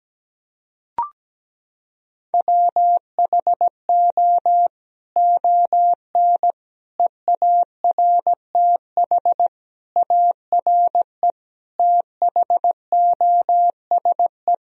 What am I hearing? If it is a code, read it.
Morse code, heard as a keyed tone, about 17 wpm, 700 Hz. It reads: WHO ON EARTH ARE THOSE
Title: Sentences from Top 1000 Words - Review Part 1 - 17wpm